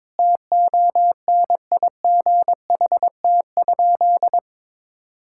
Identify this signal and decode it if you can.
Morse code, heard as a keyed tone, about 22 wpm, 700 Hz. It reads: TONIGHT?
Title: Sentences from Top 1000 Words - Review Part 6 - 22wpm